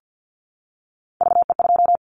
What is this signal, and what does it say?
Morse code, heard as a keyed tone, about 50 wpm, 700 Hz. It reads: VE2
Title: Callsign Prefixes for Top 50 DXCC - 50wpm